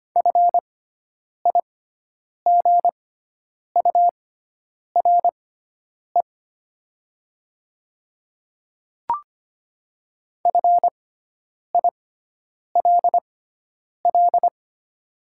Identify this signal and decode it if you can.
Morse code, heard as a keyed tone, about 25 wpm, spaced as for 9 wpm, 700 Hz. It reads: FIGURE FILL
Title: 6x Character Spacing - Top 500 Words - 25wpm